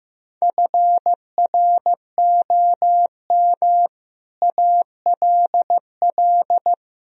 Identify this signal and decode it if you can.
Morse code, heard as a keyed tone, about 15 wpm, 700 Hz. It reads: FROM ALL
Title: Speed-Racing - Sets of 5 Words 15wpm